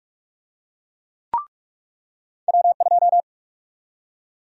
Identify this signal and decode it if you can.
Morse code, heard as a keyed tone, about 46 wpm, 700 Hz. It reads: W2
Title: USA Call Sign Prefixes - 45wpm